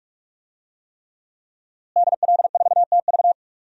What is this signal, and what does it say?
Morse code, heard as a keyed tone, about 47 wpm, 700 Hz. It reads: DL4TV
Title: DXCC-100 Call Signs - 5 Characters - Repeated 3x - 45wpm